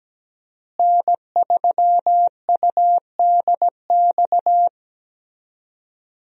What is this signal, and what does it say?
Morse code, heard as a keyed tone, about 17 wpm, 700 Hz. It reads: N3UDX
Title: USA Call Signs - 5 Characters - Repeated 3x - 17wpm